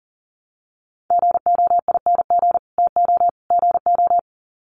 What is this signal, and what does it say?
Morse code, heard as a keyed tone, about 40 wpm, 700 Hz. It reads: GOING TO GO
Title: Sets of 3 Words 40wpm